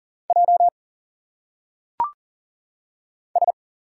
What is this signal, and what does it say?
Morse code, heard as a keyed tone, about 40 wpm, 700 Hz. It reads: J S
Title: Single Letters - Phonetic Pronunciation 40wpm